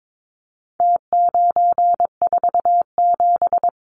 Morse code, heard as a keyed tone, about 22 wpm, 700 Hz. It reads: T947
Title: Numbers - Contest Style - 4-Digit 22wpm